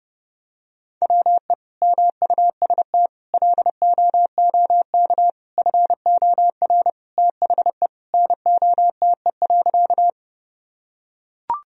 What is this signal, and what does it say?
Morse code, heard as a keyed tone, about 31 wpm, 700 Hz. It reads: WE MUST LOOK FOR THE NOTE.
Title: Sentences from Top 1000 Words - Review Part 2 - 30wpm